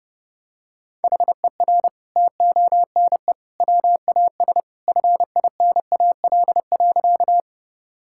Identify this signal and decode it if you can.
Morse code, heard as a keyed tone, about 32 wpm, 700 Hz. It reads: HER TONE WAS FINAL.
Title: Sentences from Top 1600 Words 30wpm